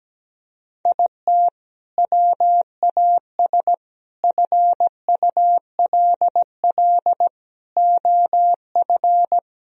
Morse code, heard as a keyed tone, about 17 wpm, 700 Hz. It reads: IT WAS FULL OF
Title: Sentences from Top 1000 Words - Review Part 3 - 17wpm